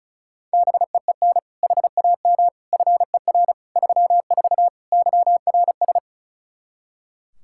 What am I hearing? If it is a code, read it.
Morse code, heard as a keyed tone, about 36 wpm, 700 Hz. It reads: BEEN HAM FER 34 YRS
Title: Practice QSO - Exchange 3 - More Info - 35wpm